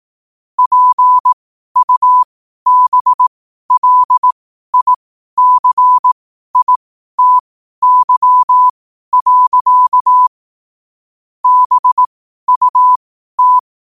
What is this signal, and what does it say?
Morse code, heard as a keyed tone, about 18 wpm, spaced as for 12 wpm, 1000 Hz. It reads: PUBLICITY. BUT
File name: qotdfiles / 12wpm